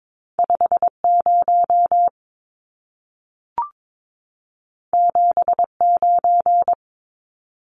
Morse code, heard as a keyed tone, about 22 wpm, 700 Hz. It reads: 50 79
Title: Numbers - 2-Digit 22wpm